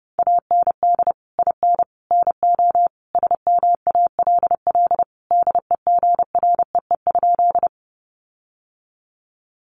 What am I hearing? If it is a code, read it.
Morse code, heard as a keyed tone, about 30 wpm, 700 Hz. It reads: AND IN NO SMALL DEGREE?
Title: Sentences from Top 1500 Words 30wpm